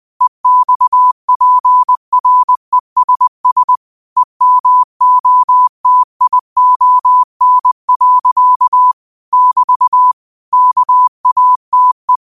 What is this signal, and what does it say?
Morse code, heard as a keyed tone, about 20 wpm, 1000 Hz. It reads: EXPRESS EMOTION. <BT> KATE
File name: qotdfiles / 20wpm